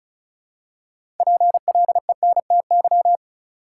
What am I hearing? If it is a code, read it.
Morse code, heard as a keyed tone, about 36 wpm, 700 Hz. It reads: PLENTY
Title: Six Letter Words - 35wpm